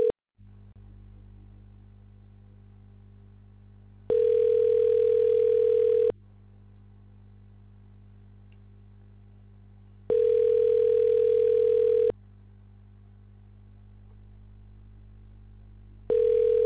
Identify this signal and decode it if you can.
Ringback tone (North America), 440 + 480 Hz.